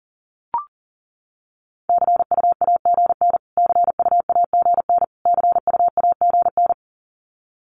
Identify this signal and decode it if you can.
Morse code, heard as a keyed tone, about 40 wpm, 700 Hz. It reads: CU AGN CU AGN CU AGN
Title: Abbreviations Repeated 3x - 40wpm